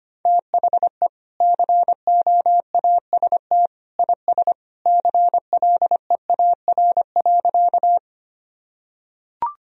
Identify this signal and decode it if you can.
Morse code, heard as a keyed tone, about 25 wpm, 700 Hz. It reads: THE COAST IS CLEAR.
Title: Sentences from Top 1000 Words - Review Part 4 - 25wpm